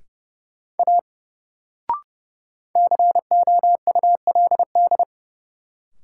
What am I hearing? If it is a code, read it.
Morse code, heard as a keyed tone, about 30 wpm, 700 Hz. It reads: A COULD
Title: Sentences from Top 100 Words - New Word Drill - 30wpm